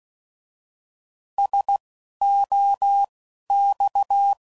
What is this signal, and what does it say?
Morse code, heard as a keyed tone, about 16 wpm, spaced as for 12 wpm, 800 Hz. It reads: SOX